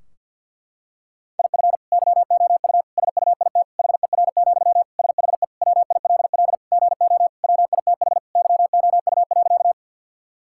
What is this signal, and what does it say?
Morse code, heard as a keyed tone, about 50 wpm, 700 Hz. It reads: IF YOU SUIT HER, SHE WILL GO WITH YOU.
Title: Sentences from Top 1600 Words 50wpm